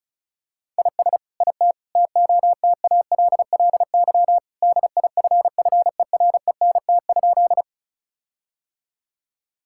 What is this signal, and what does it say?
Morse code, heard as a keyed tone, about 34 wpm, 700 Hz. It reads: IS IT TOTALLY DIFFERENT?
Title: Sentences from Top 1600 Words 35wpm